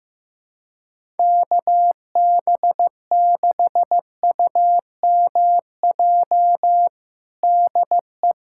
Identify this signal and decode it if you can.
Morse code, heard as a keyed tone, about 15 wpm, 700 Hz. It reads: KB6UMJ DE
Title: USA Callsign DE Callsign - 6 Characters - 15wpm